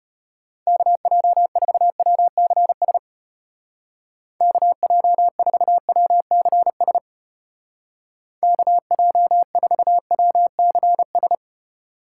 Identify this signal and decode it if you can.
Morse code, heard as a keyed tone, about 33 wpm, 700 Hz. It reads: KJ4WCS KJ4WCS KJ4WCS